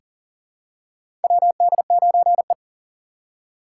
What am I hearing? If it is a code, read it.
Morse code, heard as a keyed tone, about 40 wpm, 700 Hz. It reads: WD9E
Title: USA Call Signs - 4 Characters - 2x Character Spacing - 40wpm